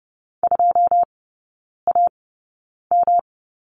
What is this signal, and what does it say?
Morse code, heard as a keyed tone, about 30 wpm, spaced as for 9 wpm, 700 Hz. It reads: 2AM
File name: Lesson 18e - Letter m - Three Characters 30wpm